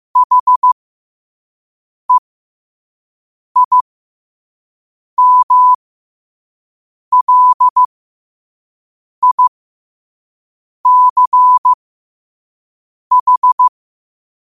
Morse code, heard as a keyed tone, about 15 wpm, spaced as for 5 wpm, 1000 Hz. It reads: HEIMLICH
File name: qotdfiles / 5wpm